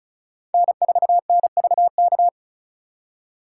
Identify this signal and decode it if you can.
Morse code, heard as a keyed tone, about 45 wpm, 700 Hz. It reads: N4NVK